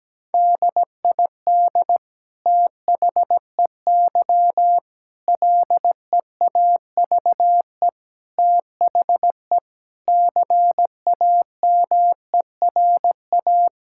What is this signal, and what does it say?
Morse code, heard as a keyed tone, about 17 wpm, 700 Hz. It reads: DID THEY LEAVE THE CAMERA
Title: Sentences from Top 1000 Words - Review Part 5 - 17wpm